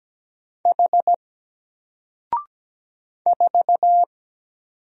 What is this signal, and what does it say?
Morse code, heard as a keyed tone, about 17 wpm, 700 Hz. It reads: H 4
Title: Single Character (Letter:Number:Punctuation:Prosign) 17wpm